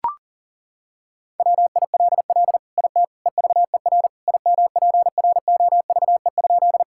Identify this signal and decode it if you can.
Morse code, heard as a keyed tone, about 40 wpm, 690 Hz. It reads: WILL IT EVER IMPROVE?
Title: Sentences from Top 1000 Words - Review Part 5 - 40wpm